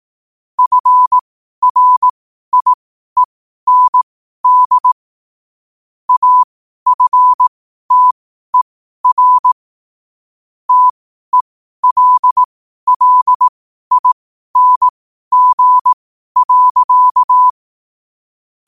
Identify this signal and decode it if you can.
Morse code, heard as a keyed tone, about 18 wpm, spaced as for 12 wpm, 1000 Hz. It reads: FRIEND AFTER TELLING.
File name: qotdfiles / 12wpm